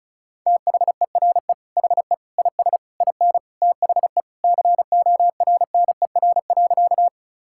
Morse code, heard as a keyed tone, about 34 wpm, 700 Hz. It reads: THERE HE IS IN THE CORNER.
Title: Sentences from Top 1600 Words 35wpm